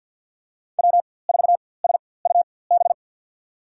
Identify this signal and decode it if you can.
Morse code, heard as a keyed tone, about 51 wpm, spaced as for 23 wpm, 700 Hz. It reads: W4SUB
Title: USA Call Signs - 5 Characters - 4x Character Spacing - 50wpm